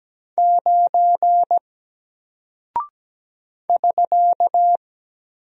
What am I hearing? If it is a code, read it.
Morse code, heard as a keyed tone, about 17 wpm, 700 Hz. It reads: 9 <SK>
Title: Single Character (Letter:Number:Punctuation:Prosign) Alt 17wpm